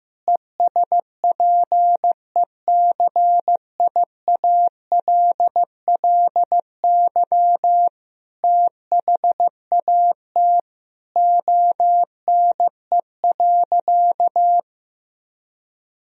Morse code, heard as a keyed tone, about 15 wpm, 700 Hz. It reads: ESPECIALLY THAT ONE.
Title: Sentences from Top 1000 Words - Review Part 4 - 15wpm